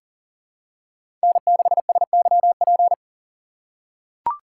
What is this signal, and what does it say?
Morse code, heard as a keyed tone, about 40 wpm, 700 Hz. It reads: N6SYP
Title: USA Call Signs - 3x Character Spacing - 40wpm